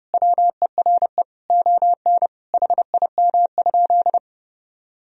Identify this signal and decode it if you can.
Morse code, heard as a keyed tone, about 30 wpm, 700 Hz. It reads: WERE ON HIM?
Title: Sentences from Top 1000 Words - Review Part 2 - 30wpm